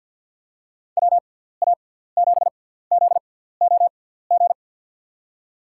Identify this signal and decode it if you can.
Morse code, heard as a keyed tone, about 49 wpm, 700 Hz. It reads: W A 7 Z O G
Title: USA Call Signs - 6 Characters - 6x Character Spacing - 50wpm